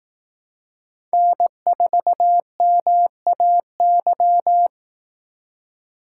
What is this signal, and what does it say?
Morse code, heard as a keyed tone, about 18 wpm, 700 Hz. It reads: N4MAY
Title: Speed Racing - USA 1x3 Call Signs 15wpm